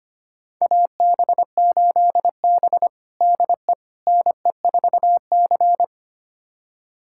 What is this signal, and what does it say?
Morse code, heard as a keyed tone, about 25 wpm, 700 Hz. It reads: AB8B DE NE4C
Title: USA Callsign DE Callsign - 4 Characters - 25wpm